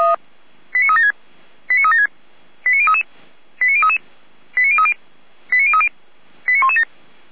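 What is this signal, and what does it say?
Five-tone selective call (ZVEI), heard as quick runs of five tones, tones of about 70 ms. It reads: 89376 89376 89033 89033 89030 89030 89117
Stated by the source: ZVEI five-tone selective call received on SDR.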